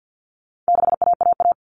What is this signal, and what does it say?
Morse code, heard as a keyed tone, about 50 wpm, 700 Hz. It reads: 6AAA
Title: Common Cut Numbers - 4-Digit 50wpm